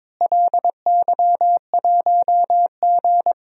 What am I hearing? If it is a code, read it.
Morse code, heard as a keyed tone, about 22 wpm, 700 Hz. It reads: LY1G
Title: DXCC-100 Call Signs - 4 Characters - Repeated 3x - 22wpm